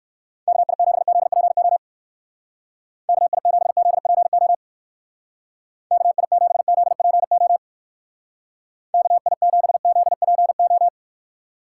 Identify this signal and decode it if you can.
Morse code, heard as a keyed tone, about 60 wpm, 700 Hz. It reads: KI7ZPO KI7ZPO KI7ZPO KI7ZPO